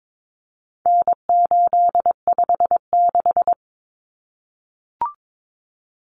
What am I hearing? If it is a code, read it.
Morse code, heard as a keyed tone, about 22 wpm, 700 Hz. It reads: N856
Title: Common Cut Numbers - 4-Digit 22wpm